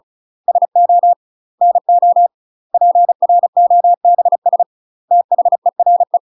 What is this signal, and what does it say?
Morse code, heard as a keyed tone, about 35 wpm, 700 Hz. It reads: SO NO PROBS THERE
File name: UK MASTER QSOs TRAINING / 35WPM / UK MASTER QSOs 0016